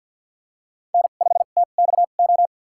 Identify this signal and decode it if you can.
Morse code, heard as a keyed tone, about 46 wpm, spaced as for 34 wpm, 700 Hz. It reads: N5TXO
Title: USA Call Signs - 5 Characters - 2x Character Spacing - 50wpm